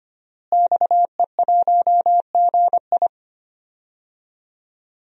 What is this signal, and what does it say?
Morse code, heard as a keyed tone, about 25 wpm, 700 Hz. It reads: XE1GI
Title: DXCC-100 Call Signs - 5 Characters - Repeated 2x - 25wpm